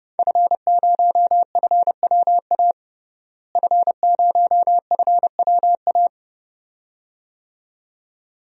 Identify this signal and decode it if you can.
Morse code, heard as a keyed tone, about 30 wpm, 700 Hz. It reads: F0FWA F0FWA
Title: DXCC-100 Call Signs - 5 Characters - Repeated 2x - 30wpm